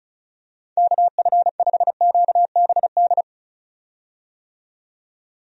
Morse code, heard as a keyed tone, about 36 wpm, 700 Hz. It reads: KF5QBD
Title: USA Call Signs - 6 Characters - 35wpm